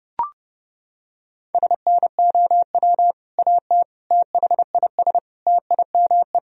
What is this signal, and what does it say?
Morse code, heard as a keyed tone, about 31 wpm, 700 Hz. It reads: SNOW AT THIS TIME
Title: Sentences from Top 1000 Words - Review Part 3 - 30wpm